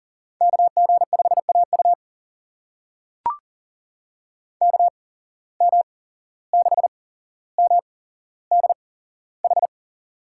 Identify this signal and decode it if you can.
Morse code, heard as a keyed tone, about 40 wpm, spaced as for 16 wpm, 700 Hz. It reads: KG5AU KM6MDH
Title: USA Call Signs - 8x Character Spacing - 40wpm